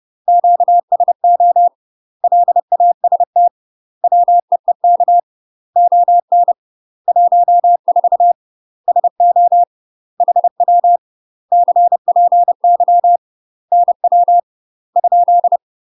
Morse code, heard as a keyed tone, about 30 wpm, 700 Hz. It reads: QSO LAST WEEK ON 14 SO HW CPY NW ?
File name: UK MASTER QSOs TRAINING / 30WPM / UK MASTER QSOs 0005